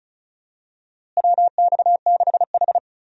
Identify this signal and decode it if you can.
Morse code, heard as a keyed tone, about 36 wpm, 700 Hz. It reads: WX6H
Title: USA Call Signs - 4 Characters - Repeated 2x - 35wpm